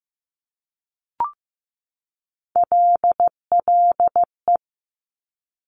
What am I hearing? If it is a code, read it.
Morse code, heard as a keyed tone, about 15 wpm, 700 Hz. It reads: LLE
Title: Top Three Letter Combinations in Words 15wpm